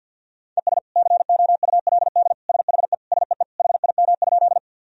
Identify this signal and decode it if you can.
Morse code, heard as a keyed tone, about 52 wpm, 700 Hz. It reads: ES COULD SHE SEE HIM?
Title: Sentences from Top 100 Words and Abbreviations 50wpm